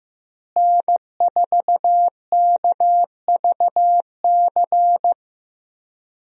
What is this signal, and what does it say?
Morse code, heard as a keyed tone, about 16 wpm, 700 Hz. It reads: N4KVC
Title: USA Call Signs - 5 Characters - Repeated 3x - 15wpm